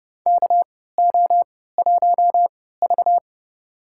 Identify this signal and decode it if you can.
Morse code, heard as a keyed tone, about 29 wpm, spaced as for 17 wpm, 700 Hz. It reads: KO1V
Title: USA Call Signs - 3x Character Spacing - 30wpm